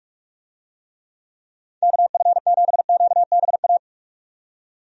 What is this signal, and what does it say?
Morse code, heard as a keyed tone, about 46 wpm, 700 Hz. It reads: KF7QBA
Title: USA Call Signs - 6 Characters - 45wpm